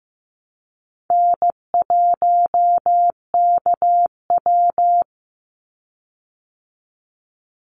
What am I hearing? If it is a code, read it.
Morse code, heard as a keyed tone, about 15 wpm, 700 Hz. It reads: N1KW
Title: Easy Call Signs Repeated 3x - 15wpm